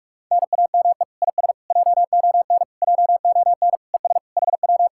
Morse code, heard as a keyed tone, about 40 wpm, 700 Hz. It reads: NAME IS JON JON ES HW